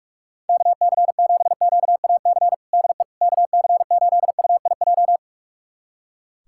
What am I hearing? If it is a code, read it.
Morse code, heard as a keyed tone, about 41 wpm, 700 Hz. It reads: KC7QAC DE KC8UIJ
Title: USA Callsign DE Callsign - 6 Characters - 45wpm